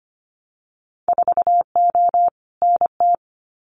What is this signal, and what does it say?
Morse code, heard as a keyed tone, about 25 wpm, 700 Hz. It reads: 4O NT
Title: Winter Field Day Exchanges Encore 25wpm